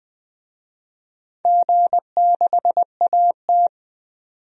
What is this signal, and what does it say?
Morse code, heard as a keyed tone, about 20 wpm, 700 Hz. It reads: G6AT
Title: DXCC-100 Call Signs - 4 Characters - 20wpm